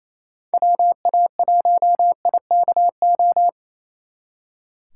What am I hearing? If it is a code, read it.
Morse code, heard as a keyed tone, about 34 wpm, 700 Hz. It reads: WA1IKO
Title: Speed Racing - USA 2x3 Call Signs 28wpm